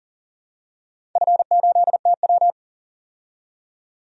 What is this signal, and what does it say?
Morse code, heard as a keyed tone, about 40 wpm, 700 Hz. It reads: F8TW
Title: DXCC-100 Call Signs - 4 Characters - 40wpm